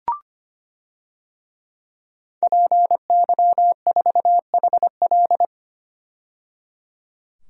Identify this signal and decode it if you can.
Morse code, heard as a keyed tone, about 25 wpm, 700 Hz. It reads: PY4HL
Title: DXCC-100 Call Signs - 5 Characters - 25wpm